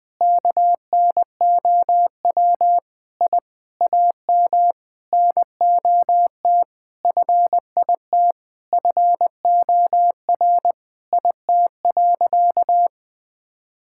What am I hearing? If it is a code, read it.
Morse code, heard as a keyed tone, about 21 wpm, 700 Hz. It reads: KNOW I AM NOT FIT FOR IT.
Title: Sentences from Top 1000 Words - Review Part 5 - 20wpm